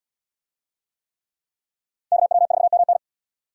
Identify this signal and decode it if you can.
Morse code, heard as a keyed tone, about 77 wpm, 700 Hz. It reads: KW4MN